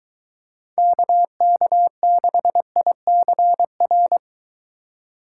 Morse code, heard as a keyed tone, about 23 wpm, 700 Hz. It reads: KK6ICR